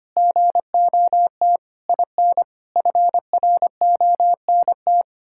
Morse code, heard as a keyed tone, about 25 wpm, 700 Hz. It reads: GOT IN FRONT
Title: Sentences from Top 1600 Words 25wpm